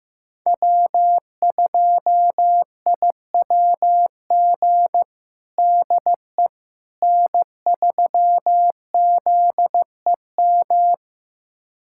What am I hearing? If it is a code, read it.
Morse code, heard as a keyed tone, about 15 wpm, 700 Hz. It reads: W2IWG DE N3ZEM
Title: USA Callsign DE Callsign - 5 Characters - 15wpm